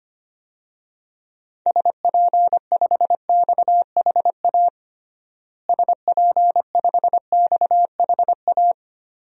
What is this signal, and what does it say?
Morse code, heard as a keyed tone, about 25 wpm, 700 Hz. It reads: SP5XHA SP5XHA
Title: DXCC-100 Call Signs - 6 Characters - Repeated 2x - 25wpm